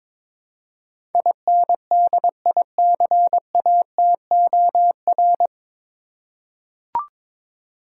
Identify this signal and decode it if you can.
Morse code, heard as a keyed tone, about 22 wpm, 700 Hz. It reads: INDICATOR
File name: 9 Letter Words - First 3 Letters Given - 22wpm